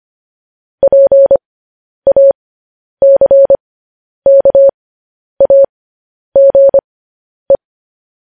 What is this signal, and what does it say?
Morse code, heard as a keyed tone, about 25 wpm, spaced as for 10 wpm, 550 Hz. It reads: PACKAGE
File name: podcast / News headlines-1x-25x10